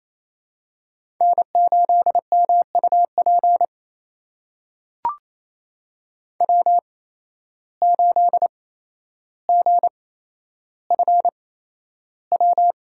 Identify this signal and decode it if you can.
Morse code, heard as a keyed tone, about 28 wpm, spaced as for 12 wpm, 700 Hz. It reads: N8MUP W8GFW
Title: USA Call Signs - 5 Characters - 8x Character Spacing - 28wpm